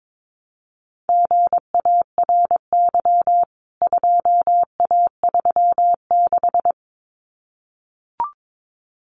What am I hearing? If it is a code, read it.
Morse code, heard as a keyed tone, about 22 wpm, 700 Hz. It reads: GARY 2A36
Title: CWT Exchange 22wpm